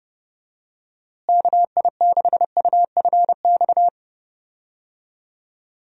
Morse code, heard as a keyed tone, about 30 wpm, 700 Hz. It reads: KI6UFX
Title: USA Call Signs - 6 Characters - Repeated 2x - 30wpm